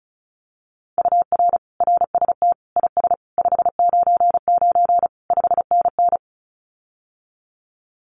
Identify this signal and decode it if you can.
Morse code, heard as a keyed tone, about 34 wpm, 700 Hz. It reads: UR RST IS 599 5NN
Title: Common QSO Phrases Encore 35wpm